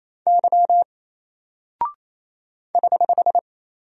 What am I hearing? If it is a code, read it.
Morse code, heard as a keyed tone, about 27 wpm, 700 Hz. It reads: Y <HH>
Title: Single Character (Letter:Number:Punctuation:Prosign) 28wpm